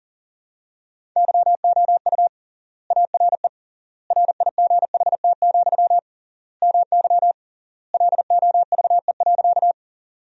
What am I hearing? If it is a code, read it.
Morse code, heard as a keyed tone, about 40 wpm, 700 Hz. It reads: YOU ARE RIGHT, MY LOVE.
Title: 3x Word-Spacing - Sentences from Top 400 Words - 40wpm